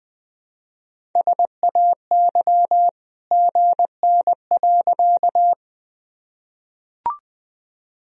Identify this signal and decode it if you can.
Morse code, heard as a keyed tone, about 20 wpm, 700 Hz. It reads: SAY GN.
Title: Sentences from Top 300 Words and Abbreviations 20wpm